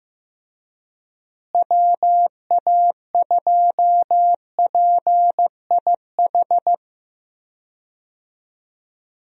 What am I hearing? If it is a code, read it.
Morse code, heard as a keyed tone, about 15 wpm, 700 Hz. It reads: WA2PIH
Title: USA Call Signs - 6 Characters - Repeated 3x - 15wpm